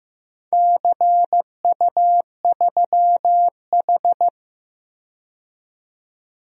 Morse code, heard as a keyed tone, about 15 wpm, 700 Hz. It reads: CU3H
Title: DXCC-100 Call Signs - 4 Characters - 15wpm